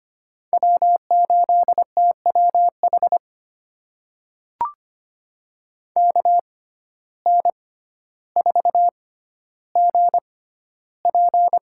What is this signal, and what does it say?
Morse code, heard as a keyed tone, about 25 wpm, spaced as for 13 wpm, 700 Hz. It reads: W8TWH KN4GP
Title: USA Call Signs - 5 Characters - 6x Character Spacing - 25wpm